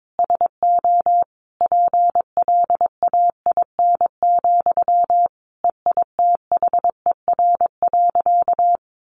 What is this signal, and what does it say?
Morse code, heard as a keyed tone, about 22 wpm, 700 Hz. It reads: SO PLAIN, EITHER.
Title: Sentences from Top 600 Words 22wpm